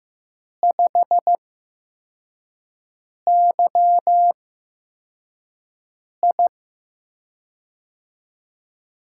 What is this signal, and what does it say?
Morse code, heard as a keyed tone, about 15 wpm, spaced as for 4 wpm, 700 Hz. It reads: 5YI
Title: USA Call Signs - 4 Characters - 8x Character Spacing - 15wpm